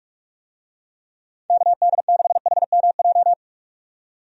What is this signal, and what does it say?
Morse code, heard as a keyed tone, about 46 wpm, 700 Hz. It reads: KD6HMJ
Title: USA Call Signs - 6 Characters - 45wpm